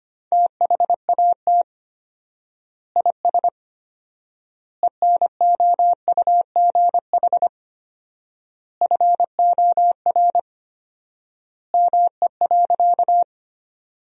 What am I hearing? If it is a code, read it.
Morse code, heard as a keyed tone, about 25 wpm, 700 Hz. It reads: THAT IS ENOUGH FOR ME.
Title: 4x Word-Spacing - Sentences from Top 400 Words - 25wpm